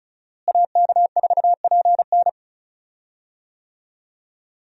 Morse code, heard as a keyed tone, about 34 wpm, 700 Hz. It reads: AK4PN
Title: USA Call Signs - 5 Characters - Repeated 2x - 35wpm